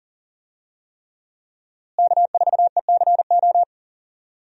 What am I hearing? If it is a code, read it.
Morse code, heard as a keyed tone, about 40 wpm, 700 Hz. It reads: K4ECO
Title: USA Call Signs - 5 Characters - Repeated 3x - 40wpm